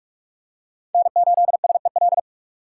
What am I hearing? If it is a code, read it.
Morse code, heard as a keyed tone, about 46 wpm, 700 Hz. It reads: N8SEL
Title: USA Call Signs - 5 Characters - 6x Character Spacing - 45wpm